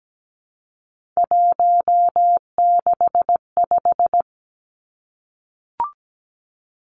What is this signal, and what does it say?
Morse code, heard as a keyed tone, about 17 wpm, 700 Hz. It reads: 165
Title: Three Challenging Letter-Number Combinations 17wpm